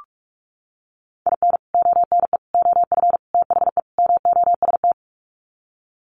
Morse code, heard as a keyed tone, about 44 wpm, 700 Hz. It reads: IN ONE OF THE MOST
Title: Sets of 5 Words Encore 45wpm